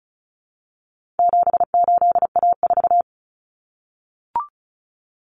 Morse code, heard as a keyed tone, about 34 wpm, 700 Hz. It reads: 78A4
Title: Common Cut Numbers - 4-Digit 35wpm